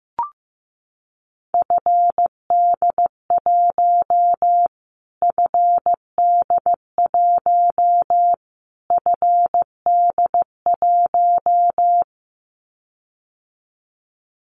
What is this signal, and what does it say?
Morse code, heard as a keyed tone, about 15 wpm, 700 Hz. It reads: FD1 FD1 FD1
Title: Callsign Prefixes for Top 25 DXCC - Repeated 3x - 15wpm